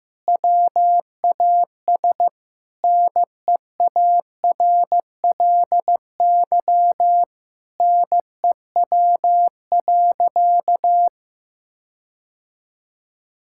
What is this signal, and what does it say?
Morse code, heard as a keyed tone, about 16 wpm, 700 Hz. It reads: WAS NEARLY NEW.
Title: Sentences from Top 1600 Words 15wpm